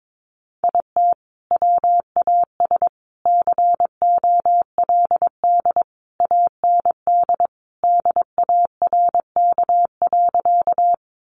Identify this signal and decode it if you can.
Morse code, heard as a keyed tone, about 22 wpm, 700 Hz. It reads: IT WAS COLD AND DARK.
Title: Sentences from Top 500 Words 22wpm